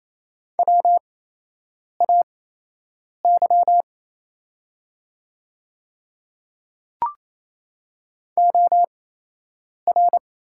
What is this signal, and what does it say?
Morse code, heard as a keyed tone, about 28 wpm, spaced as for 8 wpm, 700 Hz. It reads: WAY OR
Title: 8x Character Spacing - Top 200 Words - 28wpm